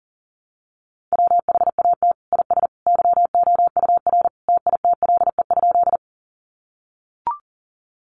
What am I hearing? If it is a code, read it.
Morse code, heard as a keyed tone, about 40 wpm, 700 Hz. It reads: WHAT IS YOUR TITLE?
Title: Sentences from Top 1200 Words 40wpm